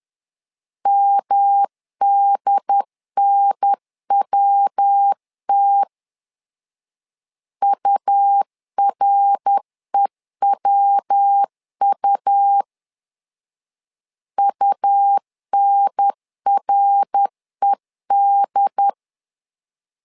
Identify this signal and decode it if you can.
Morse code, heard as a keyed tone, about 10 wpm, 800 Hz. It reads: MDNWT UREWU UNRED
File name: LEZIONI / 3 MWRO / 3 MWR 10wpm 1952